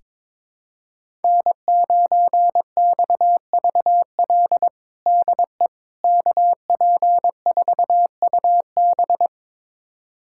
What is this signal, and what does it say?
Morse code, heard as a keyed tone, about 22 wpm, 700 Hz. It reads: N9XVL DE KP4UB
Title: USA Callsign DE Callsign 22wpm